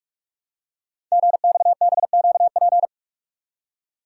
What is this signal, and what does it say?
Morse code, heard as a keyed tone, about 44 wpm, 690 Hz. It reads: GXBQP
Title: Code Groups - 5 Letters - 45wpm